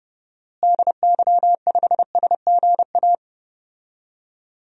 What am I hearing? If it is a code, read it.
Morse code, heard as a keyed tone, about 30 wpm, 700 Hz. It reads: DY5SGA
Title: DXCC-100 Call Signs - 6 Characters - Repeated 2x - 30wpm